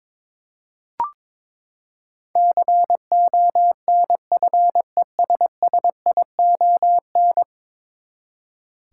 Morse code, heard as a keyed tone, about 22 wpm, 700 Hz. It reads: CONFESSION
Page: morse-code-ninja-practice-22wpm